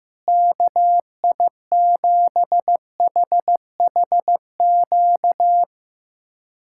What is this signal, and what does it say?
Morse code, heard as a keyed tone, about 15 wpm, 700 Hz. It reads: KI7HHQ
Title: USA Callsign DE Callsign - 6 Characters - 15wpm